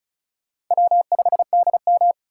Morse code, heard as a keyed tone, about 43 wpm, 700 Hz. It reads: W5DM
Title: Speed Racing - USA Call Signs 35wpm